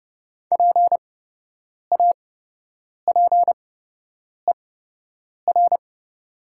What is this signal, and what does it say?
Morse code, heard as a keyed tone, about 30 wpm, spaced as for 8 wpm, 700 Hz. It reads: PAPER
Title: 8x Character Spacing - Top 300 Words - 30wpm